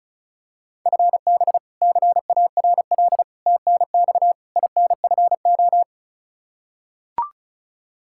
Morse code, heard as a keyed tone, about 34 wpm, 700 Hz. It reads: FB CARL TNX INFO
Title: Practice QSO - Exchange 3 - More Info - 35wpm